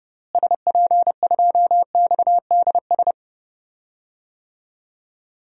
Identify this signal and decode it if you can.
Morse code, heard as a keyed tone, about 30 wpm, 700 Hz. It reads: SP2XDS
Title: DXCC-100 Call Signs - 6 Characters - Repeated 2x - 30wpm